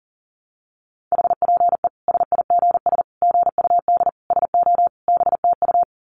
Morse code, heard as a keyed tone, about 40 wpm, 700 Hz. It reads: HPE SIGS GUD SO BTU
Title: Common QSO Phrases 40wpm